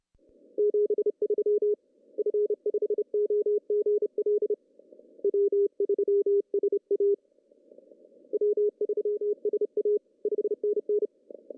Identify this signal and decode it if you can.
Morse code, heard as a keyed tone, about 30 wpm, 410 Hz. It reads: 73 F5OGL W3SA W3SA 5NN
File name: CW / cqww-cw-k5tr